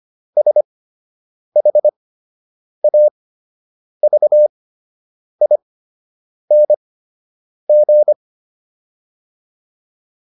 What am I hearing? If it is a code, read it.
Morse code, heard as a keyed tone, about 25 wpm, spaced as for 8 wpm, 600 Hz. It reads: SHAVING